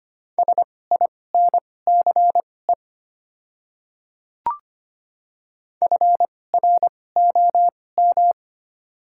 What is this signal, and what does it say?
Morse code, heard as a keyed tone, about 25 wpm, spaced as for 18 wpm, 700 Hz. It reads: SINCE FROM
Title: 2x Character Spacing - Top 500 Words - 25wpm